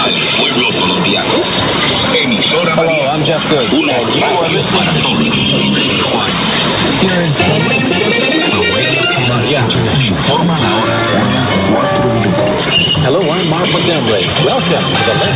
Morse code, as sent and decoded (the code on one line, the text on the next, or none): none